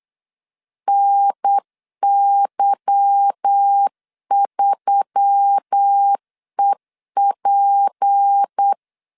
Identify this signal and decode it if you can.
Morse code, heard as a keyed tone, about 8 wpm, 800 Hz. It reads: NY3EP
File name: LEZIONI / 8 YJ45 / 8 YJ 8wpm 3313